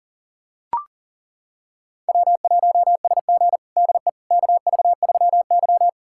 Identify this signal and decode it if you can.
Morse code, heard as a keyed tone, about 40 wpm, 700 Hz. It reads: W1SG DE KV3Y
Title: USA Callsign DE Callsign - 4 Characters - 40wpm